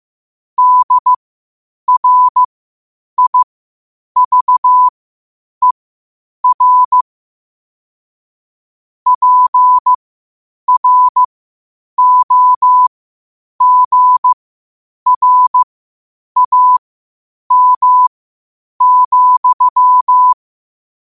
Morse code, heard as a keyed tone, about 15 wpm, spaced as for 8 wpm, 1000 Hz. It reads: DRIVER PROGRAM,
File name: qotdfiles / 8wpm